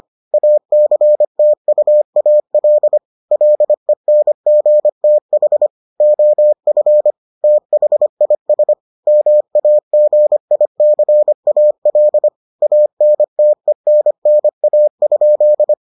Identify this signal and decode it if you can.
Morse code, heard as a keyed tone, about 25 wpm, 600 Hz. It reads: ACTUAL LENGTH OF THIS MAGICAL ANTENNA?